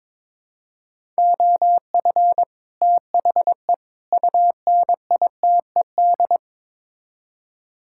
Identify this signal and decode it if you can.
Morse code, heard as a keyed tone, about 22 wpm, 700 Hz. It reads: OF THE UNITED
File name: practice / 22wpm / Build A Phrase - 5-Word Phrases - 22wpm